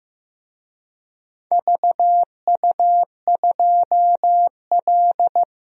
Morse code, heard as a keyed tone, about 15 wpm, 700 Hz. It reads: VU2L
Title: DXCC-100 Call Signs - 4 Characters - 15wpm